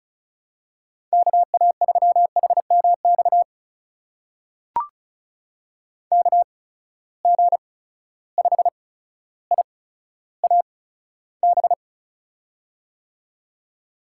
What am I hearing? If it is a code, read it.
Morse code, heard as a keyed tone, about 35 wpm, spaced as for 15 wpm, 700 Hz. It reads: KA3HMX KG5IAB
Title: USA Call Signs - 6 Characters - 8x Character Spacing - 35wpm